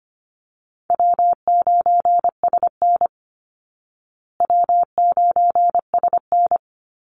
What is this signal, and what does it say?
Morse code, heard as a keyed tone, about 25 wpm, 700 Hz. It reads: W9SN W9SN
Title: Easy Call Signs Repeated 2x - 25wpm